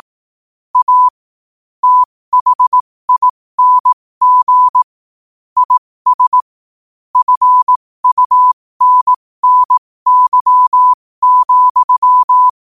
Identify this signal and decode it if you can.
Morse code, heard as a keyed tone, about 18 wpm, spaced as for 15 wpm, 1000 Hz. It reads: A THING IS FUNNY,
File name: qotdfiles / 15wpm